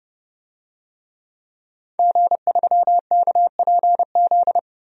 Morse code, heard as a keyed tone, about 32 wpm, 700 Hz. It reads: G3KPZ
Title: DXCC-100 Call Signs - 5 Characters - 30wpm